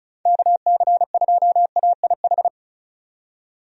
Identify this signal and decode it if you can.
Morse code, heard as a keyed tone, about 35 wpm, 700 Hz. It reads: KC2AIH
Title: USA Call Signs - 6 Characters - 35wpm